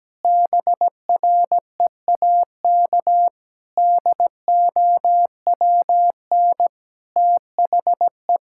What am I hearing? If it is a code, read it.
Morse code, heard as a keyed tone, about 17 wpm, 700 Hz. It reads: BREAK DOWN THE
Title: Sentences from Top 1000 Words - Review Part 6 - 17wpm